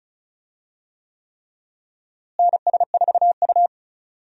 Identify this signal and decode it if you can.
Morse code, heard as a keyed tone, about 36 wpm, 700 Hz. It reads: NS4U
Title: USA Call Signs - 4 Characters - Repeated 3x - 35wpm